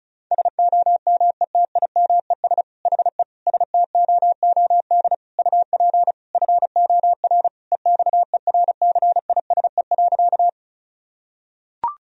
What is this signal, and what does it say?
Morse code, heard as a keyed tone, about 34 wpm, 700 Hz. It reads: SOMETIMES HE STOOD UP FOR EXERCISE.
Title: Sentences from Top 1000 Words - Review Part 4 - 35wpm